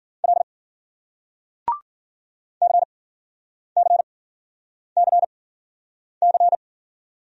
Morse code, heard as a keyed tone, about 56 wpm, spaced as for 9 wpm, 700 Hz. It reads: L CCCC